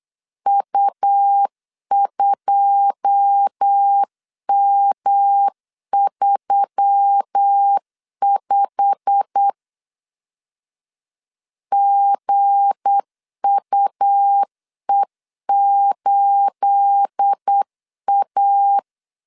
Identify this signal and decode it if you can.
Morse code, heard as a keyed tone, about 8 wpm, 800 Hz. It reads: U2M35 GUE8A
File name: LEZIONI / 9 28 8wpm 3512